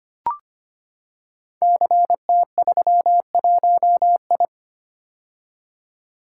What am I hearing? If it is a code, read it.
Morse code, heard as a keyed tone, about 25 wpm, 700 Hz. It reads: CT31I
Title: Code Groups - 5 Letter:Numbers - 25wpm